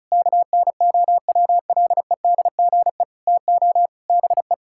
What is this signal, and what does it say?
Morse code, heard as a keyed tone, about 34 wpm, 700 Hz. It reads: KNOWLEDGE TO BE
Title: Sentences from Top 1000 Words - Review Part 6 - 35wpm